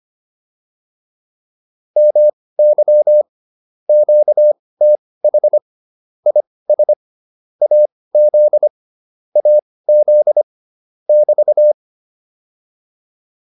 Morse code, heard as a keyed tone, about 25 wpm, spaced as for 18 wpm, 600 Hz. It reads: MY QTH IS AZ AZ <BT>
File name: QSO209 18